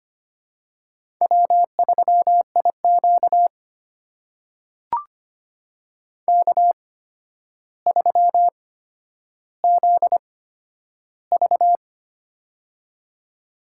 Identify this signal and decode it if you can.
Morse code, heard as a keyed tone, about 25 wpm, spaced as for 11 wpm, 700 Hz. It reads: W3IQ K3ZV
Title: USA Call Signs - 4 Characters - 8x Character Spacing - 25wpm